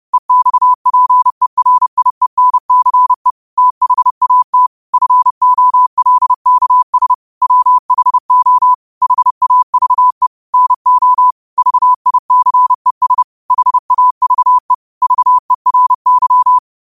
Morse code, heard as a keyed tone, about 30 wpm, 1000 Hz. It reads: EXPERIENCE THAT FOLKS WHO HAVE NO VICES HAVE VERY
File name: qotdfiles / 30wpm